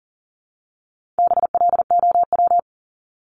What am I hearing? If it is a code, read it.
Morse code, heard as a keyed tone, about 40 wpm, 700 Hz. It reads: BLOW
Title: Top 1751-2000 Words 40wpm